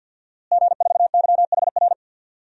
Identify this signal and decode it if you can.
Morse code, heard as a keyed tone, about 48 wpm, 700 Hz. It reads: G4YHR
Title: DXCC-100 Call Signs - 5 Characters - Repeated 3x - 50wpm